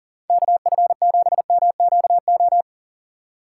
Morse code, heard as a keyed tone, about 40 wpm, 700 Hz. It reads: KF7MQO
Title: USA Call Signs - 6 Characters - 2x Character Spacing - 40wpm